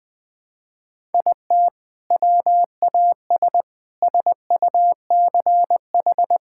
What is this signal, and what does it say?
Morse code, heard as a keyed tone, about 20 wpm, 700 Hz. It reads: IT WAS SUCH
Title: Sentences with 2-Word Phrases - 20wpm